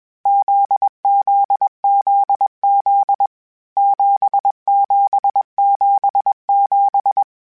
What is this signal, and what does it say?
Morse code, heard as a keyed tone, about 21 wpm, 800 Hz. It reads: ZZZZ 7777
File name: Basic Sets / Basic Letter Groups Collective7 / JZ17@20